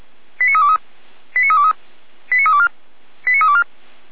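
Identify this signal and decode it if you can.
Five-tone selective call (ZVEI), heard as quick runs of five tones, tones of about 70 ms. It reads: 89323 89323 89324 89324
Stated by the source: ZVEI five-tone selective call received on an SDR.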